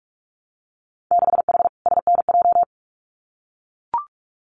Morse code, heard as a keyed tone, about 44 wpm, 700 Hz. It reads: 6H SNJ
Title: Winter Field Day Exchanges Encore 45wpm